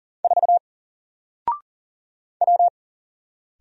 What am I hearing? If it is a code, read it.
Morse code, heard as a keyed tone, about 40 wpm, 700 Hz. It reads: 4 W
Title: Single Letter-Number Alt 40wpm